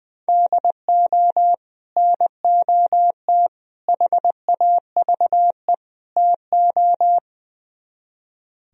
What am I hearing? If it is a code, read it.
Morse code, heard as a keyed tone, about 22 wpm, 700 Hz. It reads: DO NOT HAVE TO
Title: Speed-Racing - Sets of 4 Words 20wpm